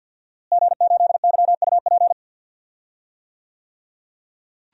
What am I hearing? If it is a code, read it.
Morse code, heard as a keyed tone, about 52 wpm, 700 Hz. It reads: G8YUP
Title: DXCC-100 Call Signs - 5 Characters - Repeated 2x - 50wpm